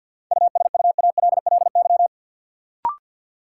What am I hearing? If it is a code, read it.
Morse code, heard as a keyed tone, about 53 wpm, 700 Hz. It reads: USUALLY
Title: Sentences from Top 1400 Words - New Word Drill Alt - 50wpm